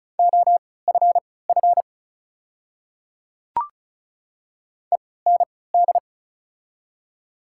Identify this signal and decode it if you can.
Morse code, heard as a keyed tone, about 36 wpm, spaced as for 20 wpm, 700 Hz. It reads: OFF END
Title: 3x Character Spacing - Top 200 Words - 35wpm